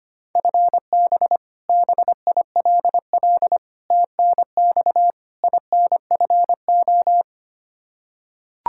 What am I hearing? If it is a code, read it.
Morse code, heard as a keyed tone, about 25 wpm, 700 Hz. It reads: FB BILL TNX INFO
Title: Practice QSO - Exchange 3 - More Info - 25wpm